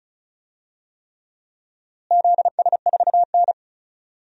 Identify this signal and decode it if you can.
Morse code, heard as a keyed tone, about 34 wpm, 700 Hz. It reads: ZS4N
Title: DXCC-100 Call Signs - 4 Characters - Repeated 3x - 35wpm